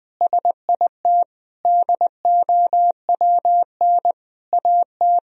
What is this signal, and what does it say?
Morse code, heard as a keyed tone, about 20 wpm, 700 Hz. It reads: SIT DOWN AT
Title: Sentences from Top 1000 Words - Review Part 2 - 20wpm